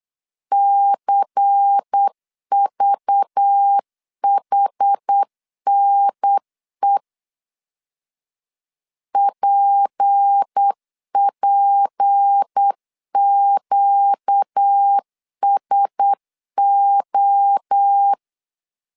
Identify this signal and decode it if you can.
Morse code, heard as a keyed tone, about 8 wpm, 800 Hz. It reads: CVHNE PPQSO
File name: LEZIONI / 6 PKX0 / 6 P 8wpm 3025